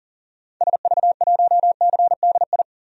Morse code, heard as a keyed tone, about 40 wpm, 700 Hz. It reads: SV1CDI
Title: DXCC-100 Call Signs - 6 Characters - 40wpm